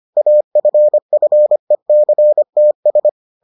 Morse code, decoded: AFFECTS

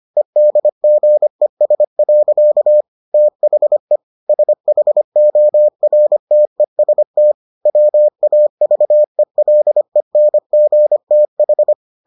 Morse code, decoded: EDGES. THE SHORTEST WAVELENGTH

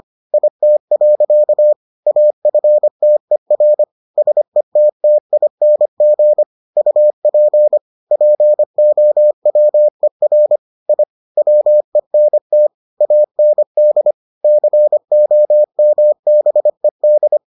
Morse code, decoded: IT. AFTER SETTING UP POWER I WENT AND COMBED

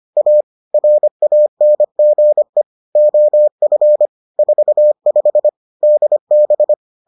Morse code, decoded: A RANGE OF 45 DB

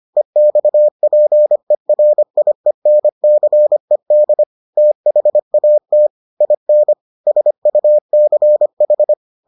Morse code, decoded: EXPERIENCED THAT IN SUCH

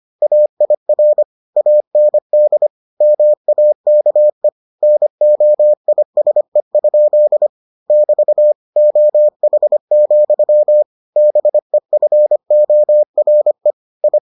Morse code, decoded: AIR AND MAKE NOISE? <BT> OH, BEFORE I

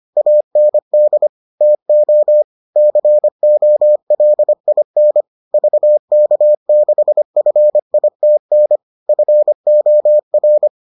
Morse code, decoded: AND TO COLIN VK6FITN FOR

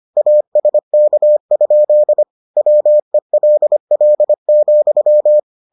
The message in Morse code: ASK? WELL,

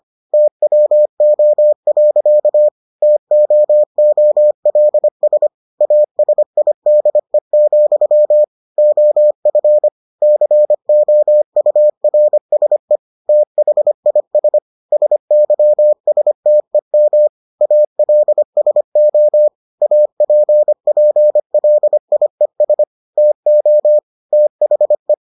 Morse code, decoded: TWO. TOOLS ASIDE, OF COURSE THIS SYSTEM ALSO APPLIES TO THE

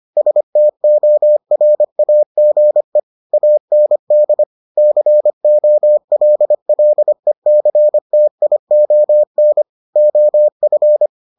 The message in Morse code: STORAGE AND COLLECTION OF

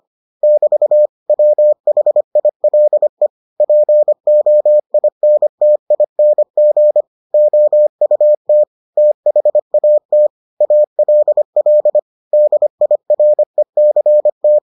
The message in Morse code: <BT> WHILE POINTING OUT THAT ALL DIRECT

